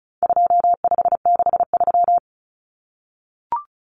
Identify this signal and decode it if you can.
Morse code, heard as a keyed tone, about 34 wpm, 700 Hz. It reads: 2563
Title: Common Cut Numbers - 4-Digit 35wpm